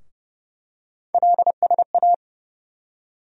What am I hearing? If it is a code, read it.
Morse code, heard as a keyed tone, about 32 wpm, 700 Hz. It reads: LSA
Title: USA Call Sign Postfixes - 3 Letters 30wpm